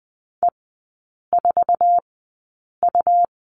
Morse code, heard as a keyed tone, about 20 wpm, spaced as for 8 wpm, 700 Hz. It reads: E4U
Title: 100 Lesson 17e - Letter u - Three Characters 20wpm